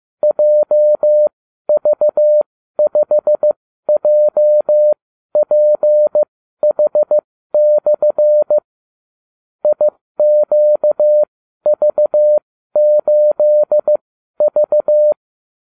Morse code, heard as a keyed tone, about 15 wpm, spaced as for 12 wpm, 600 Hz. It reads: JV5JPH/ IQV8V
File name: cw / ubacw29 2